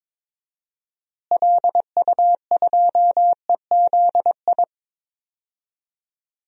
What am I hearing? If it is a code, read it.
Morse code, heard as a keyed tone, about 22 wpm, 700 Hz. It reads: LU2EZI
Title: DXCC-100 Call Signs - 6 Characters - Repeated 2x - 22wpm